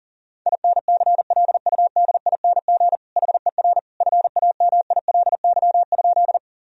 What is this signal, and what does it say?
Morse code, heard as a keyed tone, about 37 wpm, 700 Hz. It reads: INCLUDING HER FAMILY?
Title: Sentences from Top 1000 Words - Review Part 4 - 40wpm